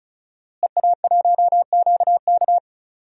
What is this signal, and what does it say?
Morse code, heard as a keyed tone, about 34 wpm, 700 Hz. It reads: EA1QK
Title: DXCC-100 Call Signs - 5 Characters - Repeated 3x - 35wpm